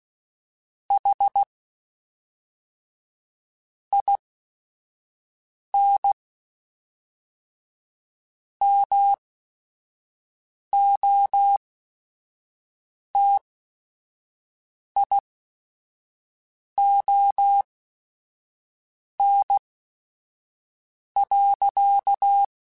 Morse code, heard as a keyed tone, about 16 wpm, spaced as for 5 wpm, 800 Hz. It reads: HIN MOTION.
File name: Chuckles / Chuckles-mp3s / BrainMouth @5